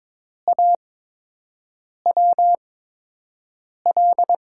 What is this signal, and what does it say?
Morse code, heard as a keyed tone, about 22 wpm, spaced as for 6 wpm, 700 Hz. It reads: AWL
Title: USA Call Signs - 8x Character Spacing - 22wpm